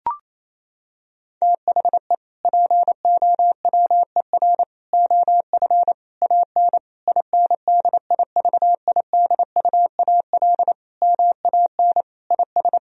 Morse code, heard as a keyed tone, about 28 wpm, 700 Hz. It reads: THE POWER OF AN INDIVIDUAL MAN IS
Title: Sentences from Top 1000 Words - Review Part 4 - 28wpm